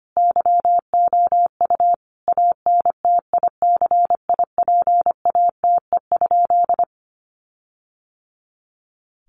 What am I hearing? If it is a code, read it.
Morse code, heard as a keyed tone, about 25 wpm, 700 Hz. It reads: YOU ANTICIPATE?
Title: Sentences from Top 1500 Words 25wpm